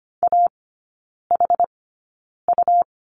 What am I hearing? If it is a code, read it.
Morse code, heard as a keyed tone, about 25 wpm, spaced as for 9 wpm, 700 Hz. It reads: AHU
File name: Lesson 17e - Letter u - Three Characters 25wpm